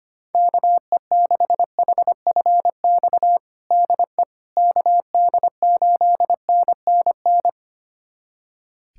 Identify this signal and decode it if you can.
Morse code, heard as a keyed tone, about 25 wpm, 700 Hz. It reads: KE6HFX DE KD8NNN
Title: USA Callsign DE Callsign - 6 Characters - 25wpm